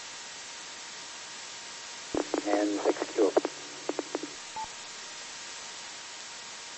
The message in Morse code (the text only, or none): none